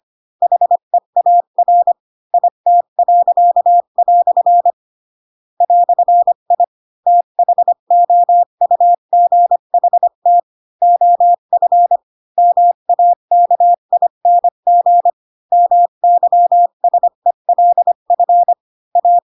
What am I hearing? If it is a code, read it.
Morse code, heard as a keyed tone, about 25 wpm, 700 Hz. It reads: HEAR IT." "I THOUGHT OF MAKING MYSELF A